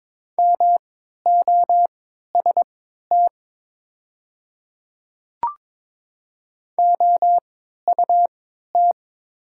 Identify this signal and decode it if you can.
Morse code, heard as a keyed tone, about 22 wpm, spaced as for 13 wpm, 690 Hz. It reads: MOST OUT
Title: 3x Character Spacing - Top 200 Words - 22wpm